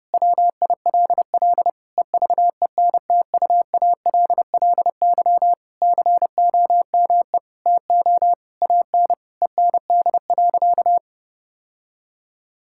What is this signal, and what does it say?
Morse code, heard as a keyed tone, about 28 wpm, 700 Hz. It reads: WILL EVENTUALLY COME TO AN END.
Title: Sentences from Top 1000 Words - Review Part 6 - 30wpm